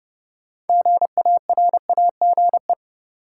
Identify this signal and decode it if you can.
Morse code, heard as a keyed tone, about 30 wpm, 700 Hz. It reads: GARAGE
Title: Sentences from Top 1600 Words - New Word Drill Alt - 30wpm